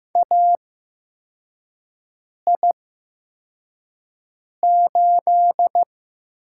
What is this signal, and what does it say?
Morse code, heard as a keyed tone, about 15 wpm, spaced as for 4 wpm, 700 Hz. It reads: AI8
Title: USA Call Signs - 4 Characters - 8x Character Spacing - 15wpm